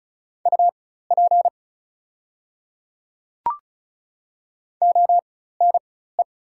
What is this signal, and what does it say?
Morse code, heard as a keyed tone, about 35 wpm, spaced as for 16 wpm, 690 Hz. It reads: UP ONE
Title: 4x Character Spacing - Top 100 Words - 35wpm